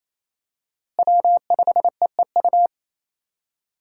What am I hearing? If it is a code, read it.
Morse code, heard as a keyed tone, about 28 wpm, 700 Hz. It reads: W5EEU
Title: USA Call Signs - 5 Characters - Repeated 2x - 28wpm